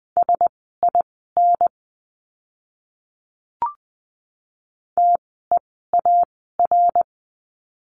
Morse code, heard as a keyed tone, about 20 wpm, spaced as for 14 wpm, 700 Hz. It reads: SIN TEAR
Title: 061 Lesson 11f - Letter h - Words 20wpm